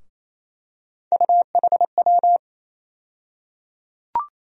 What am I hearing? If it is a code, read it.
Morse code, heard as a keyed tone, about 28 wpm, 700 Hz. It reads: UHW
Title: USA Call Sign Postfixes - 3 Letters 28wpm